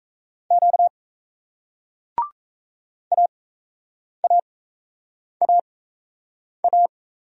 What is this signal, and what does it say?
Morse code, heard as a keyed tone, about 38 wpm, spaced as for 8 wpm, 700 Hz. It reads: Q AAAA